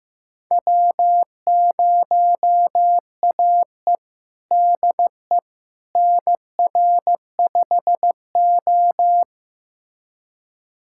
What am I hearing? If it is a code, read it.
Morse code, heard as a keyed tone, about 15 wpm, 700 Hz. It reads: W0AE DE NR5O
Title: USA Callsign DE Callsign - 4 Characters - 15wpm